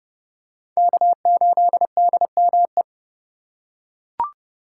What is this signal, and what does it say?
Morse code, heard as a keyed tone, about 29 wpm, 700 Hz. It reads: K8DME